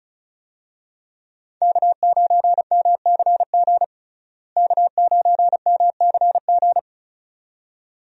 Morse code, heard as a keyed tone, about 36 wpm, 700 Hz. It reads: K9MCG K9MCG
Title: USA Call Signs - 5 Characters - Repeated 2x - 35wpm